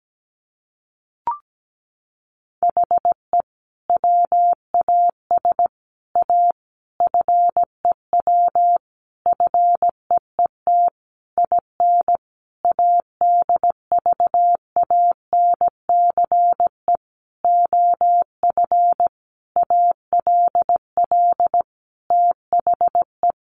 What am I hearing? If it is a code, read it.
Morse code, heard as a keyed tone, about 17 wpm, 700 Hz. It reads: HE WAS A FEW FEET IN ADVANCE OF ALL THE
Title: Sentences from Top 1400 Words 17wpm